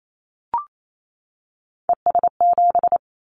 Morse code, heard as a keyed tone, about 28 wpm, 700 Hz. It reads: ES7
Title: Callsign Prefixes for Top 50 DXCC - 28wpm